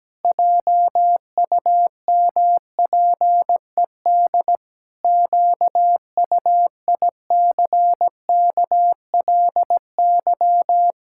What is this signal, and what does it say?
Morse code, heard as a keyed tone, about 17 wpm, 700 Hz. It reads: JUMPED QUICKLY
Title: Pangram Challenge v2 - 17wpm